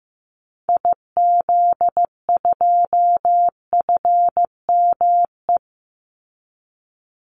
Morse code, heard as a keyed tone, about 15 wpm, 700 Hz. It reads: IZ2FME
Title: Call Signs Repeated 2x - 15wpm